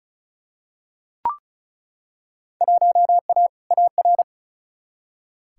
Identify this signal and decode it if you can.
Morse code, heard as a keyed tone, about 36 wpm, 700 Hz. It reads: 1A AR
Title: Field Day Exchanges Encore - 35wpm